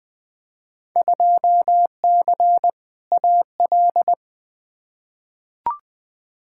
Morse code, heard as a keyed tone, about 20 wpm, 700 Hz. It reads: 2C AL
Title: Field Day Exchanges Encore - 20wpm